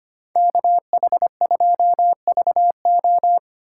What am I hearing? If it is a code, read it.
Morse code, heard as a keyed tone, about 25 wpm, 700 Hz. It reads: KH2VO
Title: USA Call Signs - 5 Characters - 2x Character Spacing - 25wpm